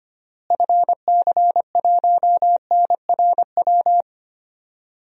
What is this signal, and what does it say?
Morse code, heard as a keyed tone, about 25 wpm, 700 Hz. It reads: FC1NRW
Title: DXCC-100 Call Signs - 6 Characters - Repeated 3x - 25wpm